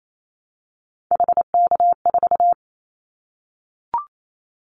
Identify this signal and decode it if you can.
Morse code, heard as a keyed tone, about 28 wpm, 700 Hz. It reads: HK4
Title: Callsign Prefixes for Top 100 DXCC - 28wpm